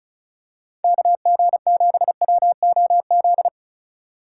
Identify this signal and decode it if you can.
Morse code, heard as a keyed tone, about 36 wpm, 700 Hz. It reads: KG7WOZ
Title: USA Call Signs - 2x Character Spacing - 35wpm